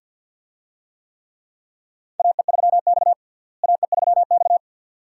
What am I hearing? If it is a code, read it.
Morse code, heard as a keyed tone, about 50 wpm, 700 Hz. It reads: AE3X AE3X
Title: USA Call Signs - 4 Characters - Repeated 2x - 50wpm